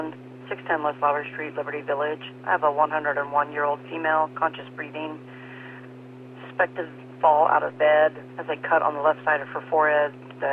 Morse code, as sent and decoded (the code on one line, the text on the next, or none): none